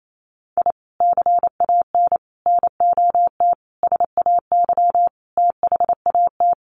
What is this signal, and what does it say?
Morse code, heard as a keyed tone, about 28 wpm, 700 Hz. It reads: I CAN NOT SAY THAT
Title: Sets of 5 Words Encore2 28wpm